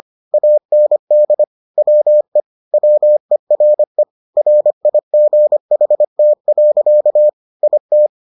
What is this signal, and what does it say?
Morse code, heard as a keyed tone, about 25 wpm, 600 Hz. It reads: AND WE WERE RIGHT. IT